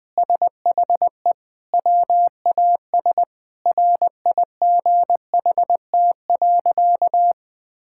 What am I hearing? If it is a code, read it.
Morse code, heard as a keyed tone, about 21 wpm, 700 Hz. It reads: SHE WAS RIGHT.
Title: Sentences with 4-Word Phrases - 20wpm